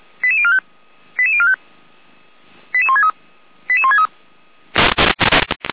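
Five-tone selective call (ZVEI), heard as quick runs of five tones, tones of about 70 ms. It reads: 88045 88045 88163 88163
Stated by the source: ZVEI five-tone call 88205 received by SDR, from fire brigade Feuerwehr Orlamünde's audio archive (f85360).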